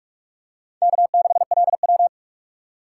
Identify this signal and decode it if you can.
Morse code, heard as a keyed tone, about 42 wpm, 700 Hz. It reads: K6LW